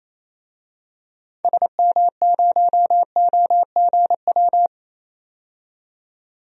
Morse code, heard as a keyed tone, about 28 wpm, 700 Hz. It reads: SM0OGW
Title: DXCC-100 Call Signs - 6 Characters - 28wpm